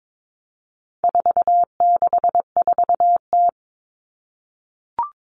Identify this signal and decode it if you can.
Morse code, heard as a keyed tone, about 22 wpm, 700 Hz. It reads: 464T
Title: All Cut Numbers - 4-Digit 22wpm